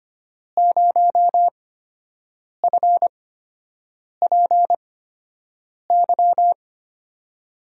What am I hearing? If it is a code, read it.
Morse code, heard as a keyed tone, about 25 wpm, spaced as for 7 wpm, 700 Hz. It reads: 0FPY
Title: USA Call Signs - 5 Characters - 8x Character Spacing - 25wpm